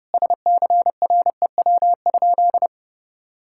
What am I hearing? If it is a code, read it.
Morse code, heard as a keyed tone, about 30 wpm, 700 Hz. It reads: SCREW?
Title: Sentences from Top 1600 Words 30wpm